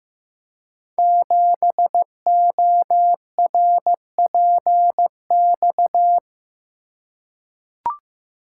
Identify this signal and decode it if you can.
Morse code, heard as a keyed tone, about 15 wpm, 700 Hz. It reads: 7ORPX
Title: Code Groups - 5 Letter:Numbers - 15wpm